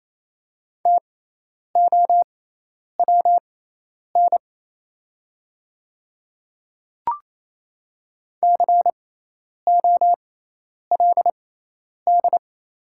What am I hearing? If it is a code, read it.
Morse code, heard as a keyed tone, about 28 wpm, spaced as for 10 wpm, 690 Hz. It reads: TOWN COLD
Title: 6x Character Spacing - Top 500 Words - 28wpm